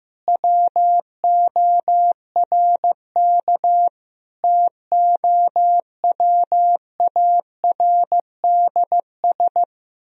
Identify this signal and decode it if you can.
Morse code, heard as a keyed tone, about 15 wpm, 700 Hz. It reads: WORK TOWARDS
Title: Sentences from Top 1000 Words - Review Part 3 - 15wpm